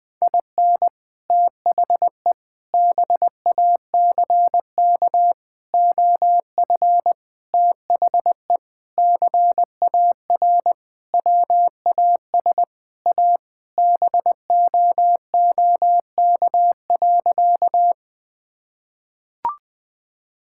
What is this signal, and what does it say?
Morse code, heard as a keyed tone, about 21 wpm, 700 Hz. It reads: IN THE BACK OF THE CAR WAS A BOOK.
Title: Sentences with 4-Word Phrases - 20wpm